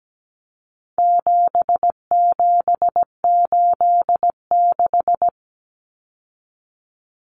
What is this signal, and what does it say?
Morse code, heard as a keyed tone, about 17 wpm, 700 Hz. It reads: 7786
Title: Numbers - 4-Digit 17wpm